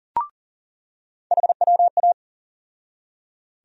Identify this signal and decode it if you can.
Morse code, heard as a keyed tone, about 40 wpm, 700 Hz. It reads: HWA
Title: USA Call Sign Postfixes - 3 Letters 40wpm